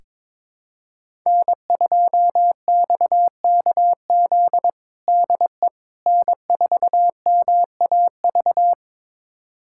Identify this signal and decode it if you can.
Morse code, heard as a keyed tone, about 22 wpm, 700 Hz. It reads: N2XKZ DE N4MAV
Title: USA Callsign DE Callsign - 5 Characters - 22wpm